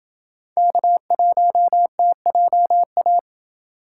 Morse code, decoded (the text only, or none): K1TJA